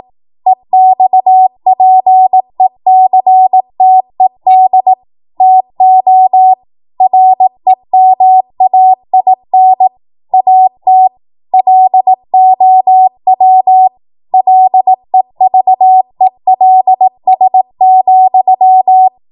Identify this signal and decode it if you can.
Morse code, heard as a keyed tone, about 18 wpm, 750 Hz. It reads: EXPECTED TO REMAIN AT LOW LEVELS,
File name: Morse / 18WPM